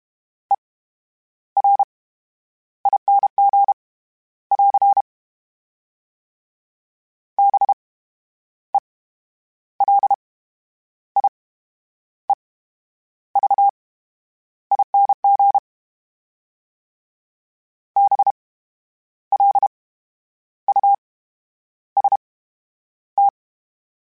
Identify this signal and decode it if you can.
Morse code, heard as a keyed tone, about 32 wpm, spaced as for 10 wpm, 800 Hz. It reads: ERING <AR> BELIEVING BLUST
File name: Special INGs / Special INGs-mp3s / VerySlowToFaster Ing @30